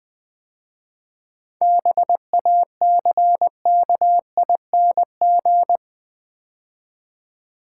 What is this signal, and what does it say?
Morse code, heard as a keyed tone, about 20 wpm, 700 Hz. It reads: BACKING
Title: 7 Letter Words - First 3 Letters Given - 20wpm